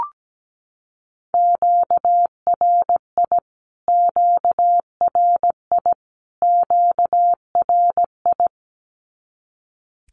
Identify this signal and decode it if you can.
Morse code, heard as a keyed tone, about 17 wpm, 700 Hz. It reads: QRI QRI QRI
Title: Q-Codes Repeated 3x 17wpm